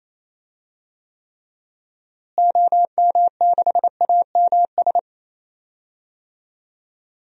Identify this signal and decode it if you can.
Morse code, heard as a keyed tone, about 28 wpm, 700 Hz. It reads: OM6AMS
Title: DXCC-100 Call Signs - 6 Characters - Repeated 2x - 28wpm